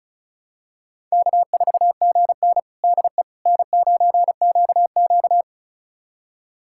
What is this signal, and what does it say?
Morse code, heard as a keyed tone, about 36 wpm, 700 Hz. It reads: K4GN DE N9QQ
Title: USA Callsign DE Callsign - 4 Characters - 35wpm